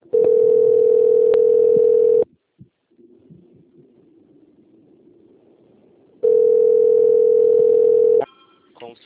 Ringback tone (North America), 440 + 480 Hz.
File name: out- - - - - .70587